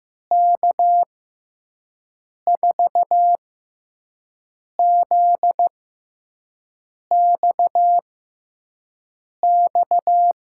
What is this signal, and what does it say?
Morse code, heard as a keyed tone, about 15 wpm, spaced as for 5 wpm, 700 Hz. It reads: K4ZXX
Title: USA Call Signs - 6 Characters - 6x Character Spacing - 15wpm